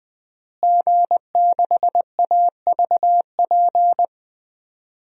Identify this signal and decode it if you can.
Morse code, heard as a keyed tone, about 20 wpm, 700 Hz. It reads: G6AVP
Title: DXCC-100 Call Signs - 5 Characters - Repeated 3x - 20wpm